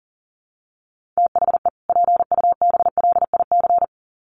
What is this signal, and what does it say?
Morse code, heard as a keyed tone, about 40 wpm, 700 Hz. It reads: THE PUBLIC
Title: Sets of 2 Words 40wpm